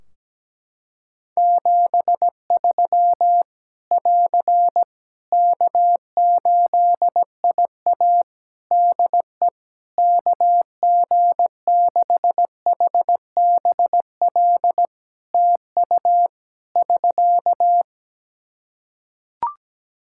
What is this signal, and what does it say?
Morse code, heard as a keyed tone, about 17 wpm, 700 Hz. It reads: 73 <AR> K8IA DE KG6HBL TU <SK>
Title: Practice QSO - Exchange 4 - Ending - 17wpm